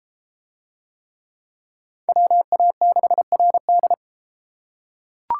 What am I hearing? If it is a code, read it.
Morse code, heard as a keyed tone, about 31 wpm, 700 Hz. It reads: WA6RD